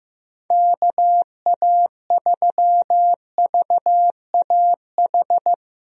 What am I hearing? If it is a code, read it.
Morse code, heard as a keyed tone, about 15 wpm, 700 Hz. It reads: KA3VAH
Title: USA Call Signs - 6 Characters - Repeated 2x - 15wpm